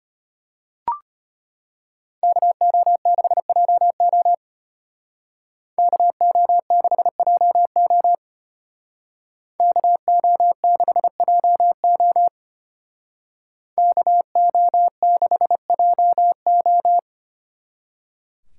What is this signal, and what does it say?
Morse code, heard as a keyed tone, about 32 wpm, 700 Hz. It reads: KO6JO KO6JO KO6JO KO6JO